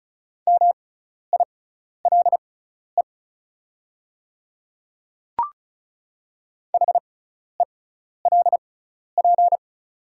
Morse code, heard as a keyed tone, about 36 wpm, spaced as for 12 wpm, 690 Hz. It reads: MILE HELP
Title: 6x Character Spacing - Top 300 Words - 35wpm